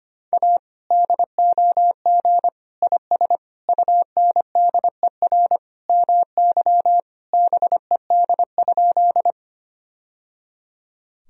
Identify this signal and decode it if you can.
Morse code, heard as a keyed tone, about 25 wpm, 700 Hz. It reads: A DOG IS UNDER MY BED?
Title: Sentences from Top 1000 Words - Review Part 2 - 25wpm